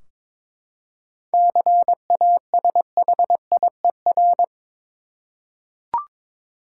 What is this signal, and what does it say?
Morse code, heard as a keyed tone, about 22 wpm, 700 Hz. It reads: CASHIER
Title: K1USN SST QSO Practice 22wpm